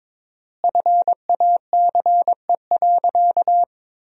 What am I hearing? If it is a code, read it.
Morse code, heard as a keyed tone, about 22 wpm, 700 Hz. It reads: FACE.
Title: 4x Word-Spacing - Sentences from Top 300 Words - 22wpm